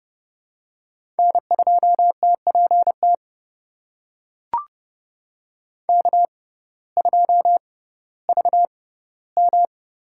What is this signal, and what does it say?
Morse code, heard as a keyed tone, about 32 wpm, spaced as for 17 wpm, 700 Hz. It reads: N2TPT K2VM
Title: USA Call Signs - 6x Character Spacing - 30wpm